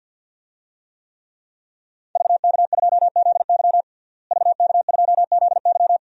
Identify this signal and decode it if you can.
Morse code, heard as a keyed tone, about 51 wpm, 700 Hz. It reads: VK2ZY VK2ZY
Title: DXCC-100 Call Signs - 5 Characters - Repeated 2x - 50wpm